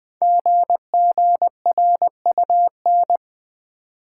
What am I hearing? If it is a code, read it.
Morse code, heard as a keyed tone, about 20 wpm, 700 Hz. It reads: GGRUN
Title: Code Groups - 5 Letters - 20wpm